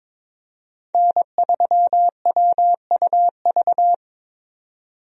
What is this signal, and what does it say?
Morse code, heard as a keyed tone, about 22 wpm, 700 Hz. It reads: N3WUV
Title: USA Call Signs - 5 Characters - 22wpm